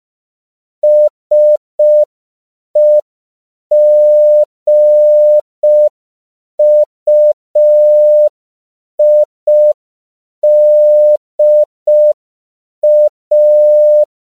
Morse code, heard as a keyed tone, about 5 wpm, 600 Hz. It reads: SEGUIDA
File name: CW / 5PPM